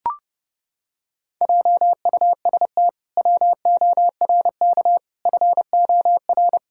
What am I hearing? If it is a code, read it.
Morse code, heard as a keyed tone, about 30 wpm, 700 Hz. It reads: JUST WORK FOR
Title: Pangram Challenge v2 - 30wpm